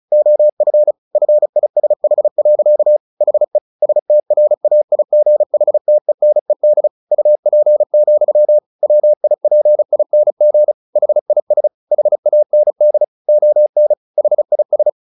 Morse code, heard as a keyed tone, about 34 wpm, 600 Hz. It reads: OF FISH. HE STRAIGHTENED UP, WIPING HIS HAND ON HIS
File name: the old man and the sea / the old man and the sea 1 35wpm